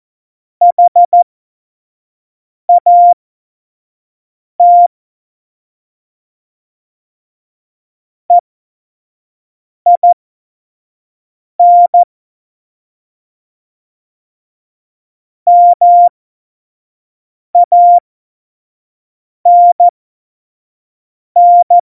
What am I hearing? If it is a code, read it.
Morse code, heard as a keyed tone, about 14 wpm, spaced as for 5 wpm, 700 Hz. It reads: HAT EIN MANN
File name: Morsekurs / Witze / Witz 14 Juli 2020, 25 F70